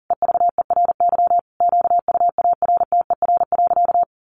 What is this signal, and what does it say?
Morse code, heard as a keyed tone, about 40 wpm, 700 Hz. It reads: EVERY QUARTER.
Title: Sentences from Top 1500 Words 40wpm